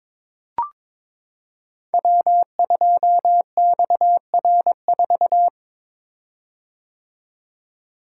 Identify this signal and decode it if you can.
Morse code, heard as a keyed tone, about 21 wpm, 700 Hz. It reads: W2XR4
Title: Code Groups - 5 Letter:Numbers - 22wpm